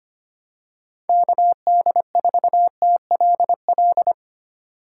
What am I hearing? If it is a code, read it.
Morse code, heard as a keyed tone, about 25 wpm, 700 Hz. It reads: KD4TLL
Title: USA Call Signs - 6 Characters - 3x Character Spacing - 25wpm